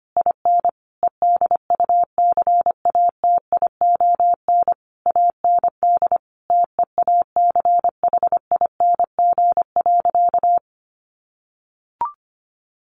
Morse code, heard as a keyed tone, about 25 wpm, 700 Hz. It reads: IN EDUCATION AND TEACHING.
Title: Sentences from Top 1100 Words 25wpm